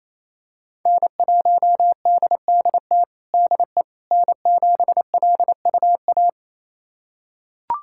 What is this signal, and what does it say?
Morse code, heard as a keyed tone, about 27 wpm, 700 Hz. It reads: N1DDT DE N7LUA
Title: USA Callsign DE Callsign - 5 Characters - 28wpm